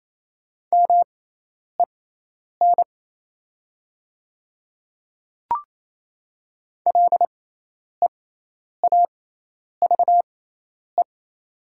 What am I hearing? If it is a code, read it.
Morse code, heard as a keyed tone, about 28 wpm, spaced as for 10 wpm, 700 Hz. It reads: MEN LEAVE
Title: 6x Character Spacing - Top 500 Words - 28wpm